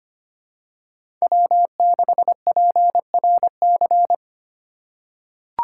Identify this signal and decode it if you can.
Morse code, heard as a keyed tone, about 25 wpm, 700 Hz. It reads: W6PRC
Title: USA Call Signs - 5 Characters - 4x Character Spacing - 25wpm